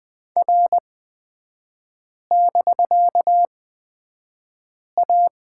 Morse code, heard as a keyed tone, about 20 wpm, spaced as for 5 wpm, 700 Hz. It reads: R<BK>A
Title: 250 Lesson 40e - BK - Three Characters 20wpm